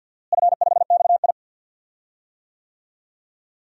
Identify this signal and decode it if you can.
Morse code, heard as a keyed tone, about 52 wpm, 700 Hz. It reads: F5XI
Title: DXCC-100 Call Signs - 4 Characters - Repeated 2x - 50wpm